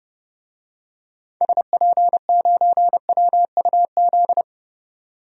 Morse code, heard as a keyed tone, about 30 wpm, 700 Hz. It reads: SP9WUZ
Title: DXCC-100 Call Signs - 6 Characters - 30wpm